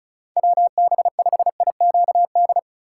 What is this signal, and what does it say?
Morse code, heard as a keyed tone, about 36 wpm, 700 Hz. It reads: WB5IQD
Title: USA Call Signs - 6 Characters - 4x Character Spacing - 35wpm